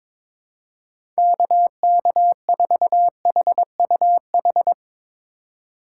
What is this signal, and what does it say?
Morse code, heard as a keyed tone, about 22 wpm, 700 Hz. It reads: KK4HUH
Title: USA Call Signs - 6 Characters - 2x Character Spacing - 22wpm